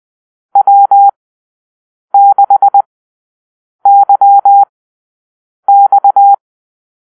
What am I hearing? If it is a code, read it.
Morse code, heard as a keyed tone, about 20 wpm, spaced as for 7 wpm, 800 Hz. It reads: W6YX
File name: arrl / Call-Sign Drill-8